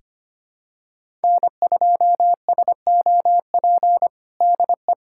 Morse code, heard as a keyed tone, about 25 wpm, 700 Hz. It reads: N2SOP DE
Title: USA Callsign DE Callsign - 5 Characters - 25wpm